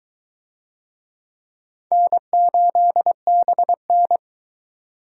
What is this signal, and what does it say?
Morse code, heard as a keyed tone, about 20 wpm, 700 Hz. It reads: N8BN